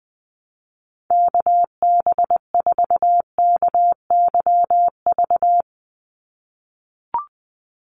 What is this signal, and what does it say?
Morse code, heard as a keyed tone, about 20 wpm, 700 Hz. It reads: KB4KYV
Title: USA Call Signs 20wpm